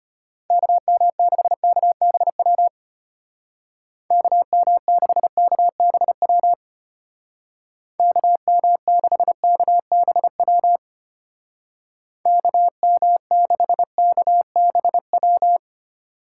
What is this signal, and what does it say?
Morse code, heard as a keyed tone, about 30 wpm, 700 Hz. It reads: KM6KBW KM6KBW KM6KBW KM6KBW
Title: Speed Racing - USA Call Signs 25wpm